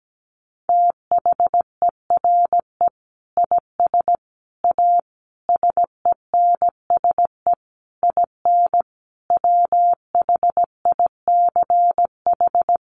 Morse code, heard as a keyed tone, about 17 wpm, 700 Hz. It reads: THERE IS A SENSE IN WHICH
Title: Sentences from Top 1500 Words 17wpm